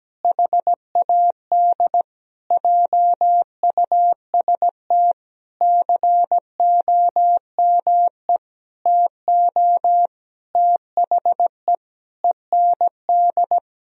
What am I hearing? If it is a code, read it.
Morse code, heard as a keyed tone, about 17 wpm, 700 Hz. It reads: HAD JUST COME TO THE END
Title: Sentences from Top 1000 Words - Review Part 2 - 17wpm